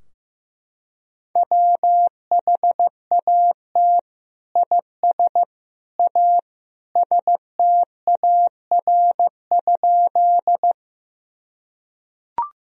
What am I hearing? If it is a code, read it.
Morse code, heard as a keyed tone, about 15 wpm, 700 Hz. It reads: WHAT IS A STAR?
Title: Sentences from Top 1000 Words - Review Part 2 - 15wpm